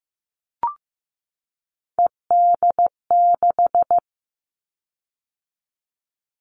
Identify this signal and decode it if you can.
Morse code, heard as a keyed tone, about 15 wpm, 700 Hz. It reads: ED6
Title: Callsign Prefixes for Top 100 DXCC - 15wpm